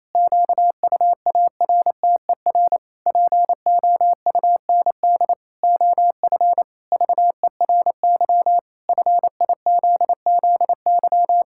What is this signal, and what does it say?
Morse code, heard as a keyed tone, about 28 wpm, 700 Hz. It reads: QUARTER POUND OF VERY FIZZY
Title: Pangram Challenge v2 - 28wpm